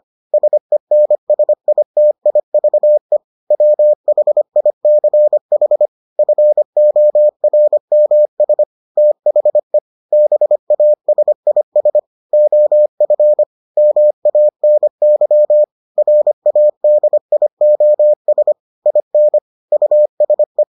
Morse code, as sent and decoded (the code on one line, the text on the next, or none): ... -. ... .. - .. ...- . / .-- .... .. -.-. .... / ..-. --- .-. -- ... / - .... / -... .- ... .. ... / --- ..-. / -- .- -. -.-- / .-. .- -.. .. --- ... / .. -. / ..- ... .
SENSITIVE WHICH FORMS THE BASIS OF MANY RADIOS IN USE